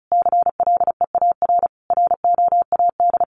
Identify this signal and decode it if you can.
Morse code, heard as a keyed tone, about 34 wpm, 700 Hz. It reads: CLEAR ROAD
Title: Sentences from Top 800 Words 35wpm